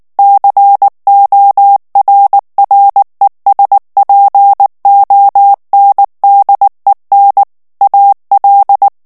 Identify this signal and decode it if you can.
Morse code, heard as a keyed tone, about 19 wpm, 800 Hz. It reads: CORRESPONDEN AL
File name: ESPAGNOL TEXTE EN CLAIR 2 VITESSE 18 MOTS MINUTES